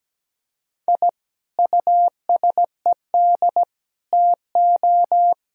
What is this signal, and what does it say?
Morse code, heard as a keyed tone, about 17 wpm, 700 Hz. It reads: I USED TO
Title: Sentences from Top 1000 Words - Review Part 2 - 17wpm